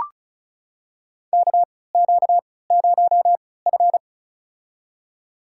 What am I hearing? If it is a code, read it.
Morse code, heard as a keyed tone, about 36 wpm, spaced as for 20 wpm, 700 Hz. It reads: KQ0F
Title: USA Call Signs - 4 Characters - 3x Character Spacing - 35wpm